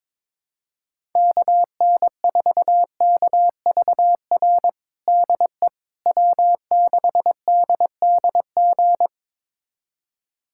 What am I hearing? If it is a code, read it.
Morse code, heard as a keyed tone, about 22 wpm, 700 Hz. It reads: KN4KVR DE W6DDG
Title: USA Callsign DE Callsign 22wpm